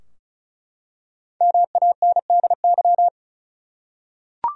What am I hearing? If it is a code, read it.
Morse code, heard as a keyed tone, about 34 wpm, 700 Hz. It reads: MANDY
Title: USA Common Names - 35wpm